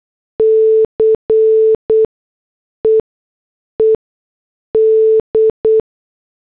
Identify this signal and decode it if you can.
Morse code, heard as a keyed tone, about 8 wpm, spaced as for 6 wpm, 440 Hz. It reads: CEED